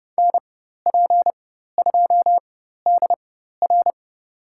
Morse code, heard as a keyed tone, about 30 wpm, spaced as for 14 wpm, 700 Hz. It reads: NP2DR
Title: USA Call Signs - 4x Character Spacing - 30wpm